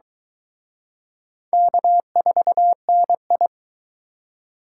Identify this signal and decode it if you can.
Morse code, heard as a keyed tone, about 23 wpm, 700 Hz. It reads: K4NI